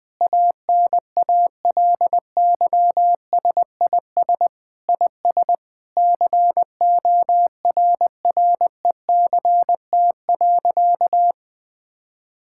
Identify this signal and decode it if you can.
Morse code, heard as a keyed tone, about 21 wpm, 700 Hz. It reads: ANALYSIS IS CORRECT.
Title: Sentences from Top 1000 Words - Review Part 5 - 20wpm